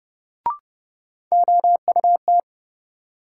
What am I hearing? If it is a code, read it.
Morse code, heard as a keyed tone, about 30 wpm, 700 Hz. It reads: OUT
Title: Sentences from Top 100 Words - New Word Drill - 30wpm